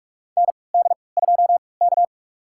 Morse code, heard as a keyed tone, about 44 wpm, spaced as for 25 wpm, 690 Hz. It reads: ND2K
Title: USA Call Signs - 4 Characters - 3x Character Spacing - 45wpm